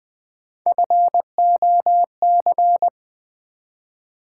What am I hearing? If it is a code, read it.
Morse code, heard as a keyed tone, about 20 wpm, 700 Hz. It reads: FOC
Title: Build A Word - 1 Letter At A Time - 5-Letter Words - 20wpm